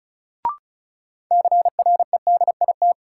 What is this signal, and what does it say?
Morse code, heard as a keyed tone, about 36 wpm, 700 Hz. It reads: CREDIT
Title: Sentences from Top 1000 Words - New Word Drill - 35wpm